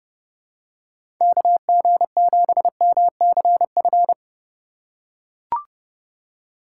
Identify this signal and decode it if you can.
Morse code, heard as a keyed tone, about 31 wpm, 700 Hz. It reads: KG7MCF
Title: USA Call Signs - 6 Characters - 3x Character Spacing - 30wpm